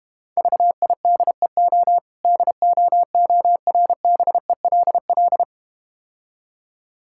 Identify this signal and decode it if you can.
Morse code, heard as a keyed tone, about 31 wpm, 700 Hz. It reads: VIDEO DOORBELL
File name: practice / 25wpm / Difficult Words - Speed-Racing with Hints - 25wpm